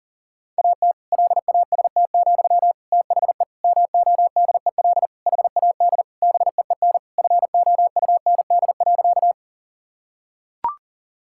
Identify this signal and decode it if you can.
Morse code, heard as a keyed tone, about 40 wpm, 700 Hz. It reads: AT LAST, THE MODEL HAD BEEN FOUND.
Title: Sentences from Top 1000 Words - Review Part 5 - 40wpm